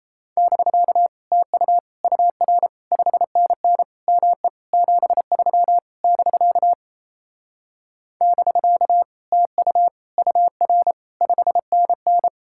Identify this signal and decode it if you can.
Morse code, heard as a keyed tone, about 37 wpm, 700 Hz. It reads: <BK> TU UR 5NN ME 73 <BK> <BK> TU UR 5NN